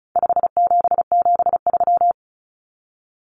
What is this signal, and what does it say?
Morse code, heard as a keyed tone, about 34 wpm, 700 Hz. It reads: 5773
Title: Common Cut Numbers - 4-Digit 35wpm